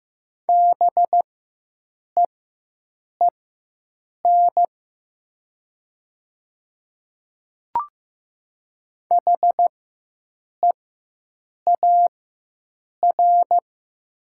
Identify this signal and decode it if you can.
Morse code, heard as a keyed tone, about 15 wpm, spaced as for 7 wpm, 700 Hz. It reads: BEEN HEAR
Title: 4x Character Spacing - Top 500 Words - 15wpm